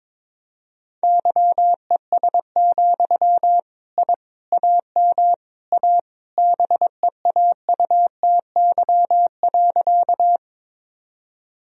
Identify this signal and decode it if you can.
Morse code, heard as a keyed tone, about 22 wpm, 700 Hz. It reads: YES, I AM A BEAUTY.
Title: Sentences from Top 1000 Words - Review Part 2 - 22wpm